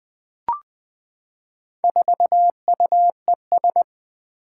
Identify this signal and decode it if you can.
Morse code, heard as a keyed tone, about 20 wpm, 700 Hz. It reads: 4UES
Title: Four Challenging Letter-Number Combinations 20wpm